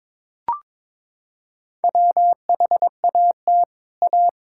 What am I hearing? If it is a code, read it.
Morse code, heard as a keyed tone, about 22 wpm, 700 Hz. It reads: WHAT A
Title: Sentences from Top 1600 Words 22wpm